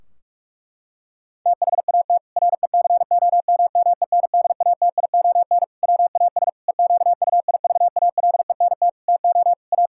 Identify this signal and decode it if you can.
Morse code, heard as a keyed tone, about 46 wpm, 700 Hz. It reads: THAT RECOMMENDATION WAS EQUIVALENT TO A
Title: Sentences from Top 1600 Words 45wpm